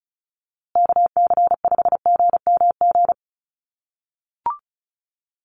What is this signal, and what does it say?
Morse code, heard as a keyed tone, about 34 wpm, 700 Hz. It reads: KC5GMG
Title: USA Call Signs Repeated 3x 35wpm